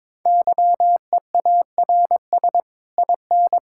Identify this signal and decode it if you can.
Morse code, heard as a keyed tone, about 22 wpm, 700 Hz. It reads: YEARS IN
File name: Sets of 9 Words - 22wpm